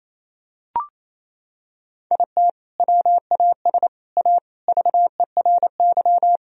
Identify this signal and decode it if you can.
Morse code, heard as a keyed tone, about 28 wpm, 700 Hz. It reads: IT WAS A VERY
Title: Sentences from Top 1000 Words - Review Part 1 - 28wpm